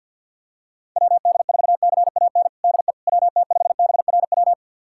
Morse code, heard as a keyed tone, about 52 wpm, 700 Hz. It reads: WD4CAN DE WT5BRW
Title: USA Callsign DE Callsign 50wpm